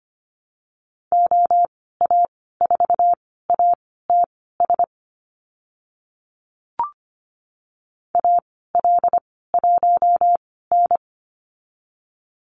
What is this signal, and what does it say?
Morse code, heard as a keyed tone, about 25 wpm, spaced as for 16 wpm, 700 Hz. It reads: OA4ATS AL1N
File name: Lesson 15g - Number 5 - Call Signs 25wpm